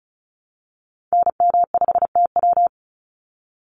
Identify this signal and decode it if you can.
Morse code, heard as a keyed tone, about 34 wpm, 700 Hz. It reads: NM5TW
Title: NA SOTA Activator Callsigns 35wpm